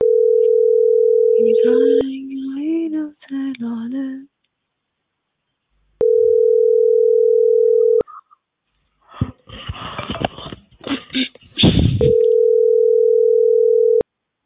Ringback tone (North America), 440 + 480 Hz.